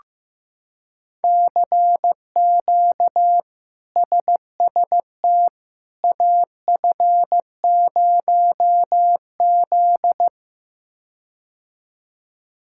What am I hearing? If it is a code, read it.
Morse code, heard as a keyed tone, about 15 wpm, 700 Hz. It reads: CQ SST AF0Z
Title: K1USN SST QSO Practice - 15wpm